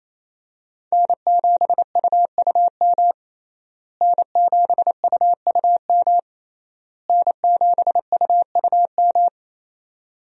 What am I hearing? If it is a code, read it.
Morse code, heard as a keyed tone, about 28 wpm, 700 Hz. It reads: N7UUM N7UUM N7UUM
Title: USA Call Signs - 5 Characters - Repeated 3x - 28wpm